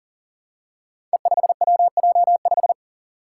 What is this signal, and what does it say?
Morse code, heard as a keyed tone, about 39 wpm, 700 Hz. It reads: E5WJ5
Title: Five Challenging Letter-Number Combinations 40wpm